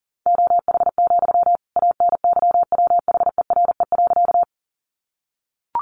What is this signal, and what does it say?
Morse code, heard as a keyed tone, about 40 wpm, 700 Hz. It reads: OH, ANYWHERE.
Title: Sentences from Top 1400 Words 40wpm